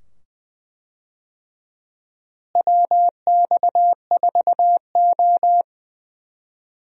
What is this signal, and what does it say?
Morse code, heard as a keyed tone, about 20 wpm, 700 Hz. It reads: WX4O
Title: USA Call Signs - 4 Characters - Repeated 2x - 20wpm